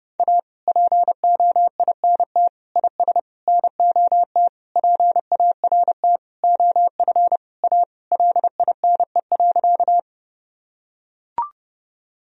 Morse code, heard as a keyed tone, about 30 wpm, 700 Hz. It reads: A POINT IS NOT PART OF A LINE.
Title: Sentences from Top 1000 Words - Review Part 1 - 30wpm